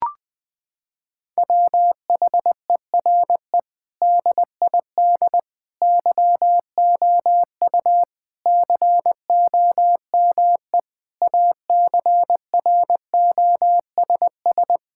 Morse code, heard as a keyed tone, about 20 wpm, 700 Hz. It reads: WHERE DID YOU COME ACROSS
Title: Sentences from Top 1000 Words - Review Part 5 - 20wpm